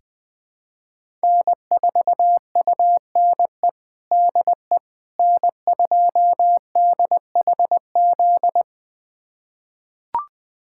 Morse code, heard as a keyed tone, about 20 wpm, 700 Hz. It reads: N4UNE DE N2DHZ
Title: USA Callsign DE Callsign - 5 Characters - 20wpm